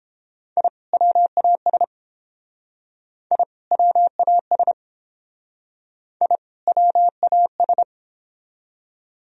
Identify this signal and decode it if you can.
Morse code, heard as a keyed tone, about 31 wpm, 700 Hz. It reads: I WAS I WAS I WAS